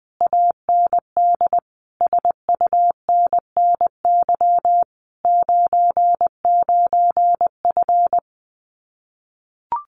Morse code, heard as a keyed tone, about 20 wpm, 700 Hz. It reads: AND SUNNY 99F
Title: Common QSO Phrases Encore 20wpm